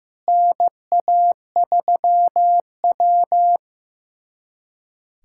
Morse code, heard as a keyed tone, about 15 wpm, 700 Hz. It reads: NA3W
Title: USA Callsign DE Callsign - 4 Characters - 15wpm